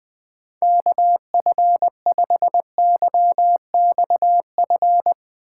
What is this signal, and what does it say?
Morse code, heard as a keyed tone, about 21 wpm, 700 Hz. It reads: KF5YXF
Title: Speed Racing - USA 2x3 Call Signs 20wpm